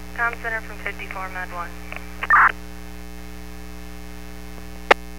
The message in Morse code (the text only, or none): none